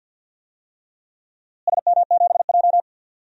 Morse code, heard as a keyed tone, about 46 wpm, 700 Hz. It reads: SM7J
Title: DXCC-100 Call Signs - 4 Characters - Repeated 3x - 50wpm